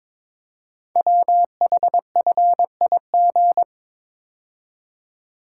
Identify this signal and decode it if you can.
Morse code, heard as a keyed tone, about 23 wpm, 700 Hz. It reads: WHFIG
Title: Code Groups - 5 Letters - 22wpm